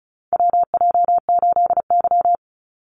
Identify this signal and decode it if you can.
Morse code, heard as a keyed tone, about 34 wpm, 700 Hz. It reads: WJ8Y
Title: Call Signs Repeated 2x - 35wpm